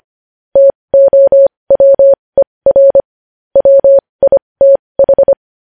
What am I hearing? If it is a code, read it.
Morse code, heard as a keyed tone, about 25 wpm, spaced as for 20 wpm, 550 Hz. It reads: TOWER WITH
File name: podcast / News headlines-1x-25x20